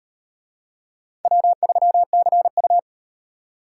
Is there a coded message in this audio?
Yes: Morse code from 1.2 to 2.8 s, heard as a keyed tone, about 38 wpm, 700 Hz. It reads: W3CU